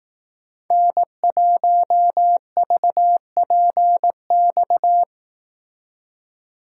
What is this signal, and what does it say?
Morse code, heard as a keyed tone, about 18 wpm, 700 Hz. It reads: N1VPX
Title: Speed Racing - USA 1x3 Call Signs 15wpm